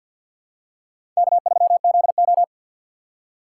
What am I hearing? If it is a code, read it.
Morse code, heard as a keyed tone, about 53 wpm, 700 Hz. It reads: K3ZO
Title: USA Call Signs - 4 Characters - 3x Character Spacing - 50wpm